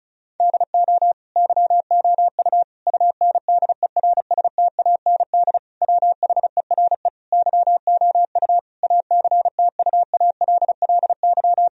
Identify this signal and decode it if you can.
Morse code, heard as a keyed tone, about 36 wpm, 700 Hz. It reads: DO YOU UNDERSTAND WHERE YOU ACTUALLY
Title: Sentences from Top 1000 Words - Review Part 4 - 35wpm